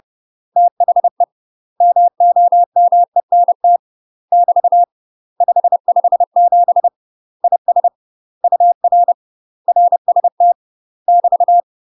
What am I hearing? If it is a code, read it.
Morse code, heard as a keyed tone, about 30 wpm, 700 Hz. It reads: THE MOMENT <BT> 557 IS UR RST <BT>
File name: UK MASTER QSOs TRAINING / 30WPM / UK MASTER QSOs 0008